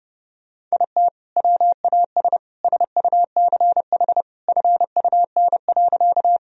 Morse code, heard as a keyed tone, about 30 wpm, 700 Hz. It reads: IT WAS SUCH FUN.
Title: Sentences from Top 1000 Words - Review Part 4 - 30wpm